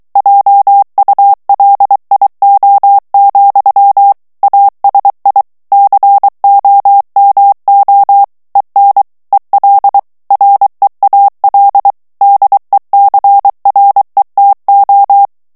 Morse code, heard as a keyed tone, about 23 wpm, 800 Hz. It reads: JULIO, ASI COMO EN EL REAL DECRETO
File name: morse mp3 / ESPAGNOL TEXTE EN CLAIR 2 VITESSE 22 MOTS MINUTES